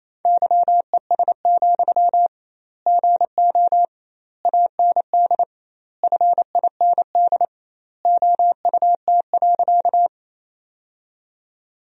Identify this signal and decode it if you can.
Morse code, heard as a keyed tone, about 28 wpm, 700 Hz. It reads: YES, GO AND FIND OUT.
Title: 2x Word-Spacing - Sentences from Top 500 Words - 28wpm